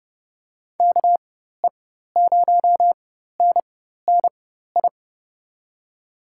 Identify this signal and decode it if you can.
Morse code, heard as a keyed tone, about 30 wpm, spaced as for 14 wpm, 700 Hz. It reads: KE0NNI
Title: USA Call Signs - 4x Character Spacing - 30wpm